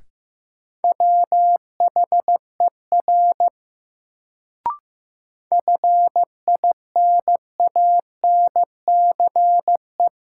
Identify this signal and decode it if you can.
Morse code, heard as a keyed tone, about 15 wpm, 700 Hz. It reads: WHER FINANCE
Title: Sentences from Top 1500 Words - New Word Drill Alt - 15wpm